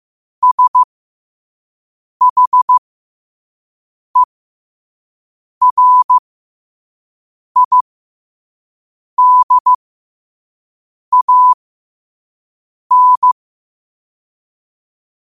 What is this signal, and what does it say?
Morse code, heard as a keyed tone, about 15 wpm, spaced as for 5 wpm, 1000 Hz. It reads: SHERIDAN
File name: qotdfiles / 5wpm